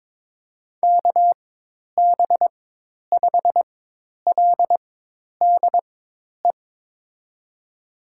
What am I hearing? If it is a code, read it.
Morse code, heard as a keyed tone, about 22 wpm, spaced as for 10 wpm, 700 Hz. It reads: KB5LDE
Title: USA Call Signs - 4x Character Spacing - 22wpm